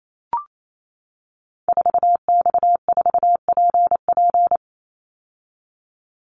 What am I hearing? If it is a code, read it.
Morse code, heard as a keyed tone, about 28 wpm, 700 Hz. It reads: 4X4PP
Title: Call Signs - 28wpm